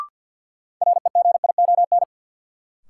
Morse code, heard as a keyed tone, about 53 wpm, 700 Hz. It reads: REGION
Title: Sentences from Top 1500 Words - New Word Drill Alt - 50wpm